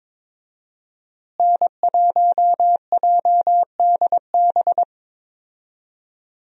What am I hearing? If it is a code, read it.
Morse code, heard as a keyed tone, about 22 wpm, 700 Hz. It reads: N1JDB
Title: USA Call Signs - 5 Characters - Repeated 2x - 22wpm